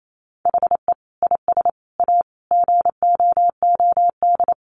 Morse code, heard as a keyed tone, about 28 wpm, 700 Hz. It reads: HE IS A GOOD
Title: Sentences from Top 1100 Words 28wpm